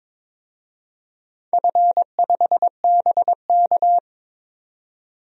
Morse code, heard as a keyed tone, about 22 wpm, 700 Hz. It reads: F5BK
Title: DXCC-100 Call Signs - 4 Characters - 22wpm